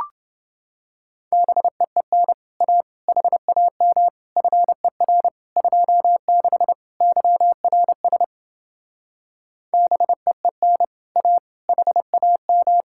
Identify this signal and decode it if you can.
Morse code, heard as a keyed tone, about 30 wpm, 700 Hz. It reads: BEEN A HAM FER 26 YRS BEEN A HAM